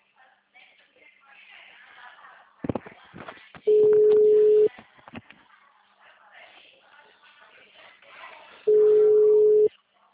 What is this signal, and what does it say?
Ringback tone (Europe, 425 Hz), with speech.